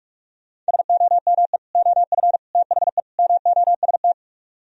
Morse code, heard as a keyed tone, about 45 wpm, 700 Hz. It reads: SOME OF THE MOST
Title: Speed-Racing - Sets of 4 Words 45wpm